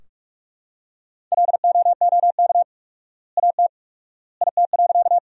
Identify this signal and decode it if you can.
Morse code, heard as a keyed tone, about 46 wpm, 700 Hz. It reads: LOOK AT IT.
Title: 4x Word-Spacing - Sentences from Top 100 Words - 45wpm